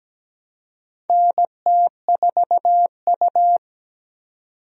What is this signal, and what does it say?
Morse code, heard as a keyed tone, about 17 wpm, 700 Hz. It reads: NT4U
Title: USA Call Signs - 4 Characters - 17wpm